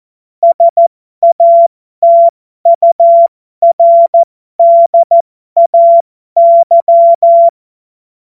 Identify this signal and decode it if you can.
Morse code, heard as a keyed tone, about 14 wpm, spaced as for 12 wpm, 680 Hz. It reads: SATURDAY
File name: Morsekurs / Witze